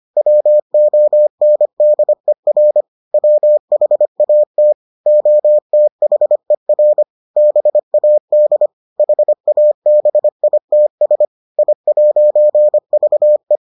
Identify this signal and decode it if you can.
Morse code, heard as a keyed tone, about 25 wpm, 600 Hz. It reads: WONDER WHAT OTHER BAD HABITS I'VE